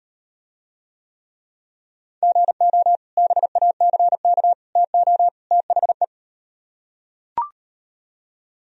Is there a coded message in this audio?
Yes: Morse code from 2.2 to 6.0 s, heard as a keyed tone, about 38 wpm, 700 Hz. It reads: GO BACK TO THE